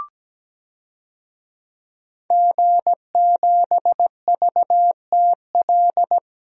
Morse code, heard as a keyed tone, about 17 wpm, 700 Hz. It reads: G7VTL
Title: DXCC-100 Call Signs - 5 Characters - 17wpm